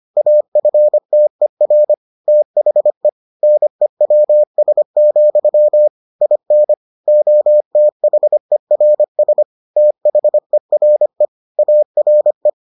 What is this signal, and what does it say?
Morse code, heard as a keyed tone, about 25 wpm, 600 Hz. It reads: AFTER THE NEWS, IN OTHERS THERE ARE